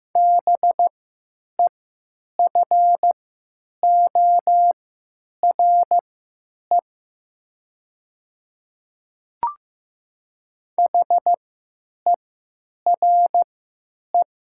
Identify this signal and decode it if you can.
Morse code, heard as a keyed tone, about 15 wpm, spaced as for 9 wpm, 700 Hz. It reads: BEFORE HERE
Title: 3x Character Spacing - Top 200 Words - 15wpm